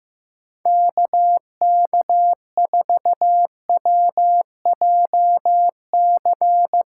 Morse code, heard as a keyed tone, about 15 wpm, 700 Hz. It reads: KK4WJC
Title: USA Callsign DE Callsign - 6 Characters - 15wpm